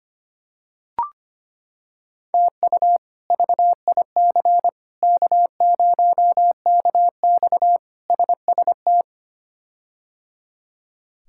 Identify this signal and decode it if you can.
Morse code, heard as a keyed tone, about 25 wpm, 700 Hz. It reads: TU VIC K0KX SST
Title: K1USN SST QSO Practice - 25wpm